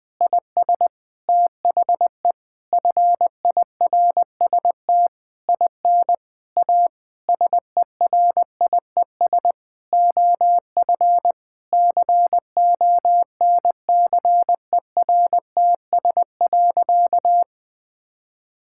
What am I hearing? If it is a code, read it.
Morse code, heard as a keyed tone, about 20 wpm, 700 Hz. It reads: IS THE FIRST IN A SERIES OF CONCERTS.
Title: Sentences with 8-Word Phrases - Spoken Beginning - 20wpm